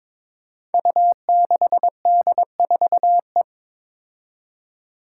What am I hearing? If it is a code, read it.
Morse code, heard as a keyed tone, about 22 wpm, 700 Hz. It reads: U6D4E
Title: Five Challenging Letter-Number Combinations 22wpm